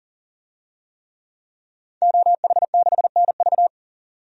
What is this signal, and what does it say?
Morse code, heard as a keyed tone, about 40 wpm, 700 Hz. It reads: OH6NV
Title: DXCC-100 Call Signs - 5 Characters - 40wpm